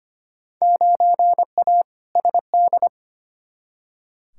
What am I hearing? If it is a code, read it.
Morse code, heard as a keyed tone, about 25 wpm, 700 Hz. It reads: 9A SD
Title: Field Day Exchanges Encore - 25wpm